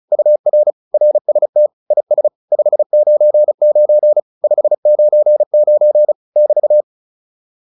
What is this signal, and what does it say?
Morse code, heard as a keyed tone, about 34 wpm, 600 Hz. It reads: UR RST IS 599 599 <BT>
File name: QSO208 35